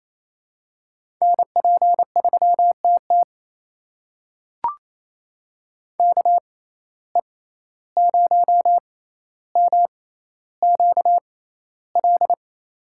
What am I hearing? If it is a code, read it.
Morse code, heard as a keyed tone, about 28 wpm, spaced as for 14 wpm, 700 Hz. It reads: NP3TT KE0MQL
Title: USA Call Signs - 6x Character Spacing - 28wpm